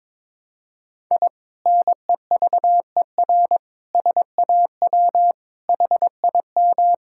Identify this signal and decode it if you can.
Morse code, heard as a keyed tone, about 22 wpm, 700 Hz. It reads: I NEVER SAW HIM
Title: Sentences from Top 1000 Words - Review Part 6 - 22wpm